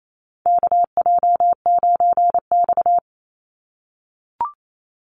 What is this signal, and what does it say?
Morse code, heard as a keyed tone, about 28 wpm, 700 Hz. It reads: KJ9X
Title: NA SOTA Chaser Callsigns 28wpm